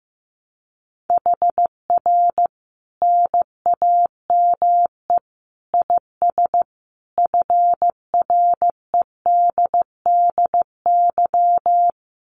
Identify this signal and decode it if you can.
Morse code, heard as a keyed tone, about 15 wpm, 700 Hz. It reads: HR NAME IS FREDDY
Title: Common QSO Phrases 15wpm